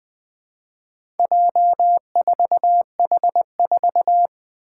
Morse code, heard as a keyed tone, about 20 wpm, 700 Hz. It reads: J4H4
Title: Four Challenging Letter-Number Combinations 20wpm